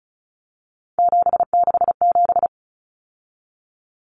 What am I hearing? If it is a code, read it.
Morse code, heard as a keyed tone, about 34 wpm, 700 Hz. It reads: 767
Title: Numbers - 3-Digit 35wpm